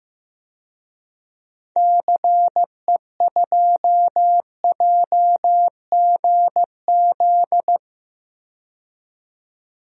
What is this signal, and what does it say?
Morse code, heard as a keyed tone, about 15 wpm, 700 Hz. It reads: CE2JGZ
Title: DXCC-100 Call Signs - 6 Characters - 15wpm